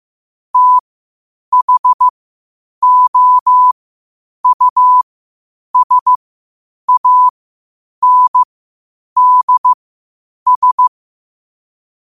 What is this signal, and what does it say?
Morse code, heard as a keyed tone, about 15 wpm, spaced as for 8 wpm, 1000 Hz. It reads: THOUSANDS
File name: qotdfiles / 8wpm